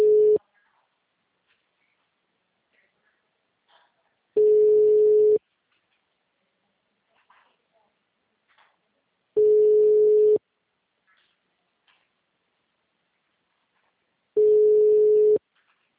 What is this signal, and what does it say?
Ringback tone (Europe, 425 Hz).